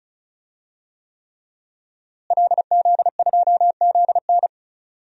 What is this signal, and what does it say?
Morse code, heard as a keyed tone, about 36 wpm, 700 Hz. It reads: LZ2ZN
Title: DXCC-100 Call Signs - 5 Characters - Repeated 3x - 35wpm